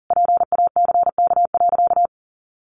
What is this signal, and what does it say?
Morse code, heard as a keyed tone, about 40 wpm, 700 Hz. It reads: PACK.
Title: Sentences from Top 1300 Words 40wpm